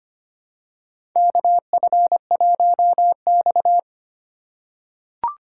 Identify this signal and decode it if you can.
Morse code, heard as a keyed tone, about 25 wpm, 700 Hz. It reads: KF1X
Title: USA Call Signs - 2x Character Spacing - 25wpm